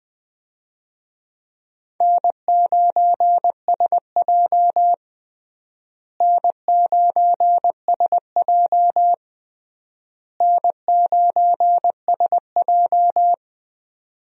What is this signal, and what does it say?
Morse code, heard as a keyed tone, about 21 wpm, 700 Hz. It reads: N9SJ N9SJ N9SJ
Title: USA Call Signs - 4 Characters - Repeated 3x - 20wpm